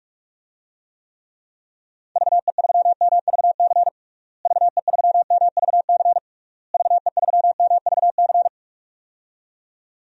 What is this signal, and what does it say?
Morse code, heard as a keyed tone, about 41 wpm, 700 Hz. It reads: VE3MVC VE3MVC VE3MVC
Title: DXCC-100 Call Signs - 6 Characters - Repeated 3x - 45wpm